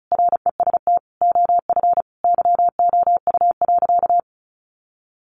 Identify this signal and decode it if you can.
Morse code, heard as a keyed tone, about 34 wpm, 700 Hz. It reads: REST OF YOU.
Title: Sentences from Top 700 Words 35wpm